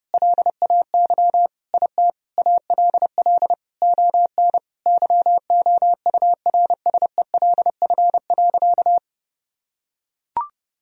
Morse code, heard as a keyed tone, about 32 wpm, 700 Hz. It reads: LAY IT ALL ON YOURSELF.
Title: Sentences from Top 1000 Words - Review Part 2 - 30wpm